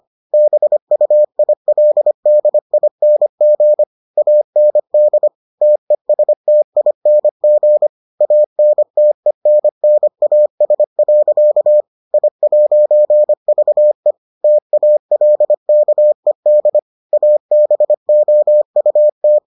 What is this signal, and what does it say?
Morse code, heard as a keyed tone, about 25 wpm, 600 Hz. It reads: BUILDING AND TESTING ANTENNAS. I'VE TALKED ABOUT